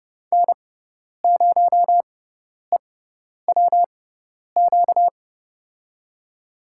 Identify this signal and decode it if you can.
Morse code, heard as a keyed tone, about 30 wpm, spaced as for 10 wpm, 700 Hz. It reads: N0EWQ
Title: USA Call Signs - 6x Character Spacing - 30wpm